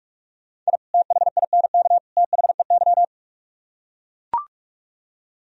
Morse code, heard as a keyed tone, about 44 wpm, 700 Hz. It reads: I THINK THEY